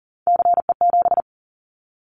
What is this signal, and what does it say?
Morse code, heard as a keyed tone, about 40 wpm, 700 Hz. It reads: CE7
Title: Callsign Prefixes for Top 100 DXCC - 40wpm